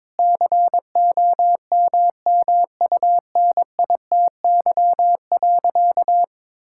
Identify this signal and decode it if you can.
Morse code, heard as a keyed tone, about 22 wpm, 700 Hz. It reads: COMMUNITY.
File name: practice / 22wpm / Sentences with 8-Word Phrases - 22wpm